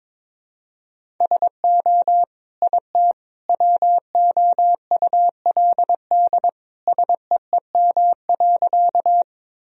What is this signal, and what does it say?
Morse code, heard as a keyed tone, about 21 wpm, 700 Hz. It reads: SO IT WOULD SEEM.
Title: Sentences from Top 1000 Words - Review Part 1 - 22wpm